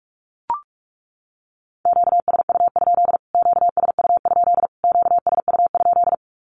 Morse code, heard as a keyed tone, about 44 wpm, 700 Hz. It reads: QSU? QSU? QSU?
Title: Q-Codes Repeated 3x 45wpm